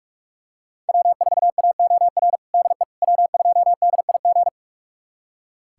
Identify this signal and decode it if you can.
Morse code, heard as a keyed tone, about 42 wpm, 700 Hz. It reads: W4AOR DE W2DIG
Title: USA Callsign DE Callsign - 5 Characters - 45wpm